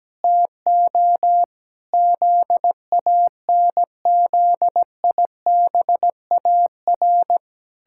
Morse code, heard as a keyed tone, about 17 wpm, 700 Hz. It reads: TO ZANZIBAR
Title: Pangram Challenge v2 - 17wpm